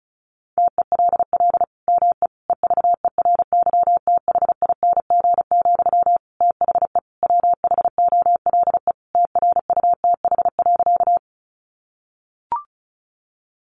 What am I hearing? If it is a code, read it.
Morse code, heard as a keyed tone, about 34 wpm, 700 Hz. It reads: TELL ME EVERYTHING, THE WHOLE TRUTH.
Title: Sentences from Top 800 Words 35wpm